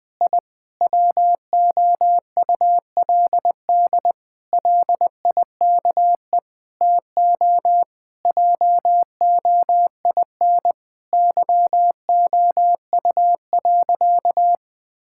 Morse code, heard as a keyed tone, about 20 wpm, 700 Hz. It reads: I WOULD LIKE TO JOIN YOU.
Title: Sentences with 5-Word Phrases - Spoken Beginning - 20wpm